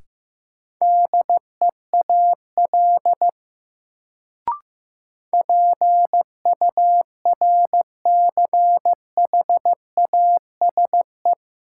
Morse code, heard as a keyed tone, about 15 wpm, 700 Hz. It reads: DEAL PURCHASE
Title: Sentences from Top 800 Words - New Word Drill - 15wpm